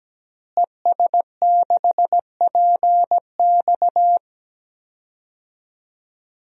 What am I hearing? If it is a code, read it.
Morse code, heard as a keyed tone, about 17 wpm, 700 Hz. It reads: ES6PX
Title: DXCC-100 Call Signs - 5 Characters - Repeated 3x - 17wpm